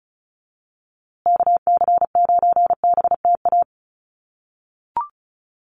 Morse code, heard as a keyed tone, about 34 wpm, 700 Hz. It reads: KC9BTA
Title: USA Call Signs Repeated 2x 35wpm